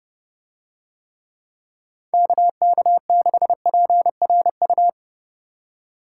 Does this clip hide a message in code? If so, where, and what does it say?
Yes: Morse code from 2.1 to 4.9 s, heard as a keyed tone, about 30 wpm, 700 Hz. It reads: KK6PRU